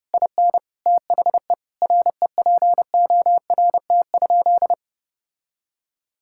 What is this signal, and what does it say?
Morse code, heard as a keyed tone, about 30 wpm, 700 Hz. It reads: IN THE REPORT?
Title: Sentences from Top 1000 Words - Review Part 6 - 30wpm